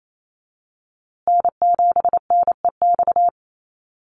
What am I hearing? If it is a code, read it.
Morse code, heard as a keyed tone, about 28 wpm, 700 Hz. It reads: N7NEX
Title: USA Call Signs Repeated 2x 28wpm